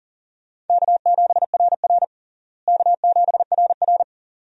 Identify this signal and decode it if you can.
Morse code, heard as a keyed tone, about 40 wpm, 700 Hz. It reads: K7RR K7RR
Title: USA Call Signs - 4 Characters - Repeated 2x - 40wpm